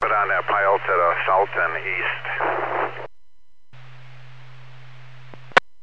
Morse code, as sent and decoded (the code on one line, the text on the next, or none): none